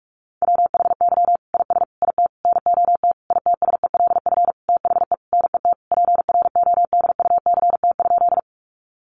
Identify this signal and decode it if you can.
Morse code, heard as a keyed tone, about 44 wpm, 700 Hz. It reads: WHY IS IT NOT ITSELF THE NET PRODUCT?
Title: Sentences from Top 1400 Words 45wpm